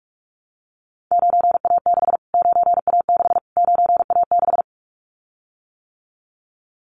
Morse code, heard as a keyed tone, about 44 wpm, 700 Hz. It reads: 9A6 9A6 9A6
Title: Callsign Prefixes for Top 25 DXCC - Repeated 3x - 45wpm